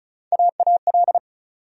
Morse code, heard as a keyed tone, about 36 wpm, 700 Hz. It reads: AAL
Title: USA Call Sign Postfixes - 3 Letters 35wpm